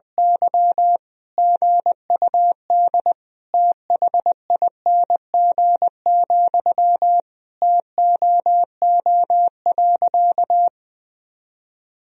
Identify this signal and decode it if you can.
Morse code, heard as a keyed tone, about 21 wpm, 700 Hz. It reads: Y GUD THING, TOO.
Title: Sentences from Top 200 Words and Abbreviations 20wpm